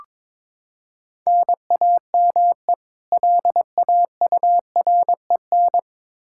Morse code, heard as a keyed tone, about 21 wpm, 700 Hz. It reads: NAME LAUREN
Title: K1USN SST QSO Practice 22wpm